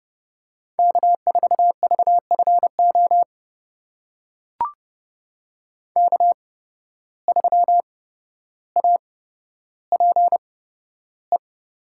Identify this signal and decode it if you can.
Morse code, heard as a keyed tone, about 30 wpm, spaced as for 13 wpm, 700 Hz. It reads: K4VFO K3APE
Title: USA Call Signs - 5 Characters - 8x Character Spacing - 30wpm